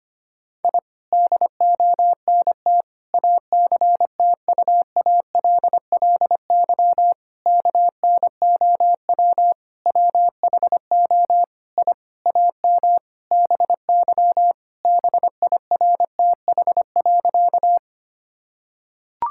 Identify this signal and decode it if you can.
Morse code, heard as a keyed tone, about 25 wpm, 700 Hz. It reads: I DONT ACTUALLY KNOW WHO I AM BY BIRTH.
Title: Sentences from Top 1000 Words - Review Part 5 - 25wpm